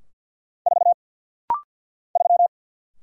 Morse code, heard as a keyed tone, about 50 wpm, 700 Hz. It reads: 4 3
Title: Single Letter-Number - Rapid-Fire 50wpm